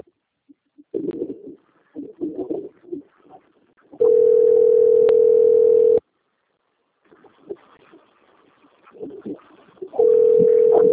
Ringback tone (North America), 440 + 480 Hz.